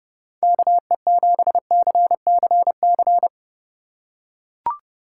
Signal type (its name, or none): Morse code